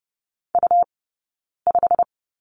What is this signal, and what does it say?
Morse code, heard as a keyed tone, about 30 wpm, spaced as for 9 wpm, 690 Hz. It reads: U5
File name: Lesson 17d - Letter u - Two Characters 30wpm